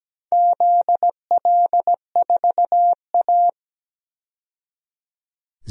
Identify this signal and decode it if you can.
Morse code, heard as a keyed tone, about 17 wpm, 700 Hz. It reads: ZL4A
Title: DXCC-100 Call Signs - 4 Characters - 17wpm